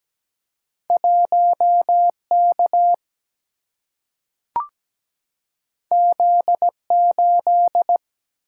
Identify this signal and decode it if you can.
Morse code, heard as a keyed tone, about 17 wpm, 700 Hz. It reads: 1K Z8
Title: Single-Letter Single-Number 17wpm